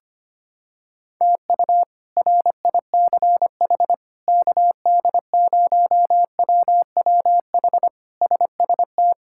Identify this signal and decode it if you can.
Morse code, heard as a keyed tone, about 25 wpm, 700 Hz. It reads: TU RICH KD0WWH SST
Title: K1USN SST QSO Practice - 25wpm